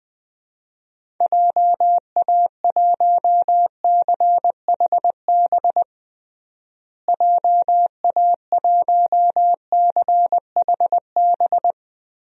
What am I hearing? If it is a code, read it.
Morse code, heard as a keyed tone, about 20 wpm, 700 Hz. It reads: JA1CHB JA1CHB
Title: DXCC-100 Call Signs - 6 Characters - Repeated 3x - 20wpm